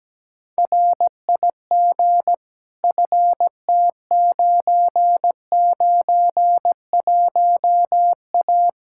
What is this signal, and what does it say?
Morse code, heard as a keyed tone, about 17 wpm, 700 Hz. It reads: RIG FT991A
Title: Practice QSO - Exchange 2 - Info - 17wpm